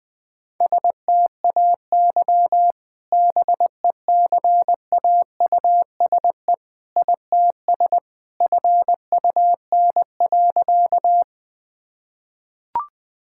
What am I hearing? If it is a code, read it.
Morse code, heard as a keyed tone, about 20 wpm, 700 Hz. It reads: STAY BECAUSE ITS FUN.
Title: Sentences with 2-Word Phrases - 20wpm